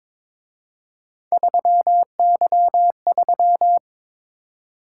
Morse code, heard as a keyed tone, about 26 wpm, 700 Hz. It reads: 3Y3
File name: Speed-Racing - Three Letter-Number Combinations - 22wpm